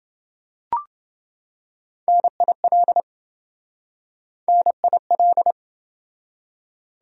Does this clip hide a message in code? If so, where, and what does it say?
Yes: Morse code from 2.1 to 5.5 s, heard as a keyed tone, about 28 wpm, 700 Hz. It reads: NIL NIL